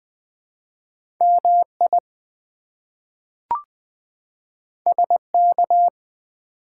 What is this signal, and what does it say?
Morse code, heard as a keyed tone, about 20 wpm, 700 Hz. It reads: MI SK
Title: ARRL Sections v2 - 20wpm